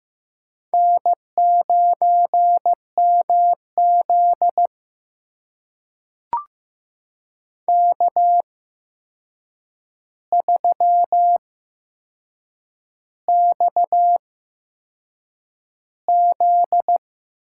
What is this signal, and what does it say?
Morse code, heard as a keyed tone, about 15 wpm, spaced as for 6 wpm, 700 Hz. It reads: N9MZ K3XZ
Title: USA Call Signs - 4 Characters - 8x Character Spacing - 15wpm